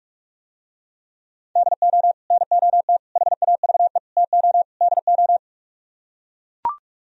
Morse code, heard as a keyed tone, about 44 wpm, 700 Hz. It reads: DO NOT HAVE TO DO